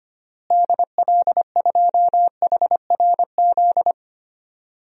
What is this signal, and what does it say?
Morse code, heard as a keyed tone, about 25 wpm, 700 Hz. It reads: DL2HRZ
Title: DXCC-100 Call Signs - 6 Characters - Repeated 3x - 25wpm